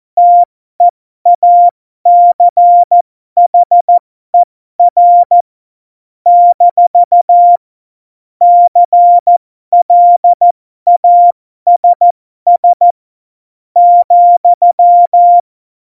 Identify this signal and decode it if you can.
Morse code, heard as a keyed tone, about 14 wpm, spaced as for 12 wpm, 700 Hz. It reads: TEACHER - CLASS ,
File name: Morsekurs / Witze